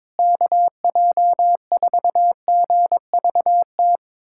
Morse code, heard as a keyed tone, about 22 wpm, 700 Hz. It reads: KJ4GVT
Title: USA Callsign DE Callsign - 6 Characters - 22wpm